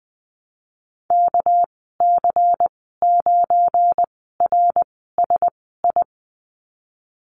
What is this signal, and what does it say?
Morse code, heard as a keyed tone, about 20 wpm, spaced as for 14 wpm, 700 Hz. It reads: KC9RSI
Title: 238 Lesson 38g - Letter q - Call Signs 20wpm